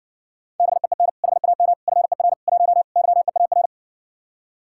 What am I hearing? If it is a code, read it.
Morse code, heard as a keyed tone, about 67 wpm, 700 Hz. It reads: BEEN HAM FER 1 YEAR